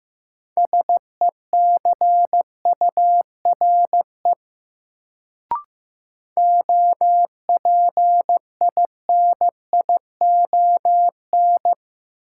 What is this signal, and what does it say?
Morse code, heard as a keyed tone, about 15 wpm, 700 Hz. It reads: SECURE OPINION
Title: Sentences from Top 1200 Words - New Word Drill - 15wpm